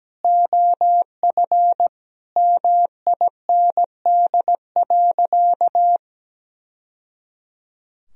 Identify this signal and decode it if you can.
Morse code, heard as a keyed tone, about 17 wpm, 700 Hz. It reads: OF MIND.
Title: Sentences from Top 1000 Words - Review Part 2 - 17wpm